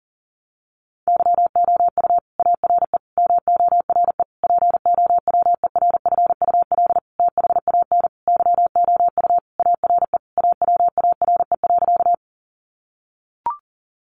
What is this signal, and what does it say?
Morse code, heard as a keyed tone, about 40 wpm, 700 Hz. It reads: YOU ARE MORE POWERFUL THAN YOU ARE AWARE.
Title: Sentences from Top 1300 Words 40wpm